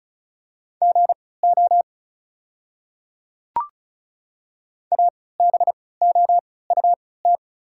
Morse code, heard as a keyed tone, about 36 wpm, spaced as for 20 wpm, 690 Hz. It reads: GO ABOUT
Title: 3x Character Spacing - Top 100 Words - 35wpm